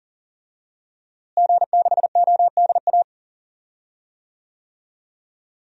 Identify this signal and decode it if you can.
Morse code, heard as a keyed tone, about 40 wpm, 700 Hz. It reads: G6ODA
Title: DXCC-100 Call Signs - 5 Characters - Repeated 2x - 40wpm